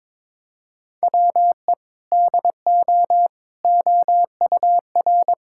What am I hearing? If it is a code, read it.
Morse code, heard as a keyed tone, about 22 wpm, 700 Hz. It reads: WE DO OUR
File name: practice / 22wpm / Sentences with 2-Word Phrases - 22wpm